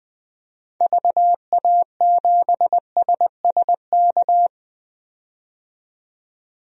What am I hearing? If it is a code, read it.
Morse code, heard as a keyed tone, about 21 wpm, 700 Hz. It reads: VA7SSK
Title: DXCC-100 Call Signs - 6 Characters - 20wpm